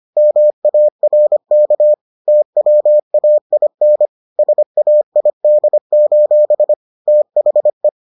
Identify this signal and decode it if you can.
Morse code, heard as a keyed tone, about 25 wpm, 600 Hz. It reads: MARK TWAIN SAID: THE